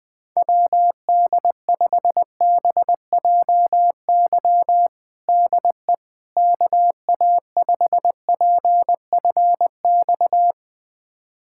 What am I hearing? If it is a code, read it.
Morse code, heard as a keyed tone, about 20 wpm, 700 Hz. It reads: WD5BJY DE KA5PFX
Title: USA Callsign DE Callsign - 6 Characters - 20wpm